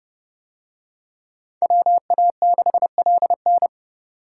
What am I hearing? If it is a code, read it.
Morse code, heard as a keyed tone, about 28 wpm, 700 Hz. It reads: WA6LN